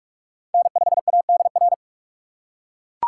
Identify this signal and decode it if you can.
Morse code, heard as a keyed tone, about 41 wpm, 700 Hz. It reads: N5ADR
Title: USA Call Signs - 5 Characters - 4x Character Spacing - 45wpm